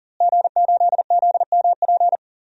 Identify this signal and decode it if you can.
Morse code, heard as a keyed tone, about 37 wpm, 700 Hz. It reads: G8ZMP
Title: DXCC-100 Call Signs - 5 Characters - Repeated 3x - 40wpm